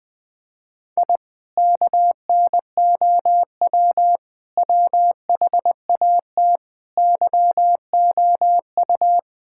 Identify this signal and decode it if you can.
Morse code, heard as a keyed tone, about 21 wpm, 700 Hz. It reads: I KNOW WHAT YOU
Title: Sentences with 2-Word Phrases - 20wpm